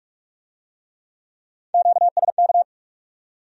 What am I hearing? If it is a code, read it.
Morse code, heard as a keyed tone, about 42 wpm, 700 Hz. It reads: QSK